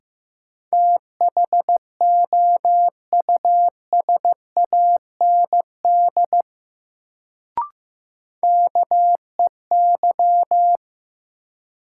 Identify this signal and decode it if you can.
Morse code, heard as a keyed tone, about 15 wpm, 700 Hz. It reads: THOUSAND KEY
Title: Sentences from Top 600 Words - New Word Drill - 15wpm